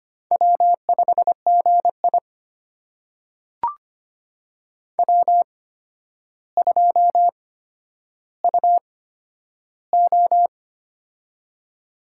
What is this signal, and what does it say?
Morse code, heard as a keyed tone, about 25 wpm, spaced as for 11 wpm, 700 Hz. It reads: W5GI W2UO
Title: USA Call Signs - 4 Characters - 8x Character Spacing - 25wpm